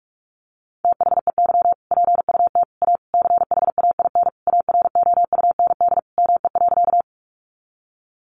Morse code, heard as a keyed tone, about 44 wpm, 700 Hz. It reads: THEY PUT A CHAIN AROUND ME.
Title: Sentences from Top 1500 Words 45wpm